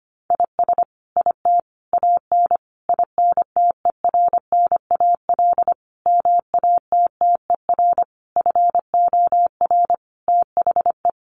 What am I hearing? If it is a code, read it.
Morse code, heard as a keyed tone, about 25 wpm, 700 Hz. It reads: IS IT AN INTERNAL MATTER FOR THE
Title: Sentences from Top 1400 Words 25wpm